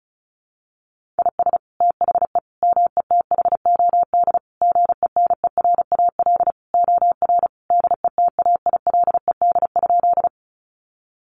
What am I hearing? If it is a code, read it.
Morse code, heard as a keyed tone, about 34 wpm, 700 Hz. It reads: IS THE METHOD GENERAL OR DETAILED?
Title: Sentences from Top 1500 Words 35wpm